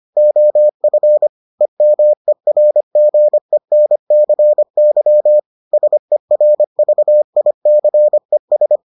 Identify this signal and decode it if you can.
Morse code, heard as a keyed tone, about 25 wpm, 600 Hz. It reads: OF EMERGENCY SERVICES